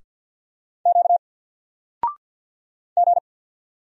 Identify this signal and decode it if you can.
Morse code, heard as a keyed tone, about 49 wpm, 700 Hz. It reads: Q G
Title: Single Character (Letter:Number:Punctuation:Prosign) 50wpm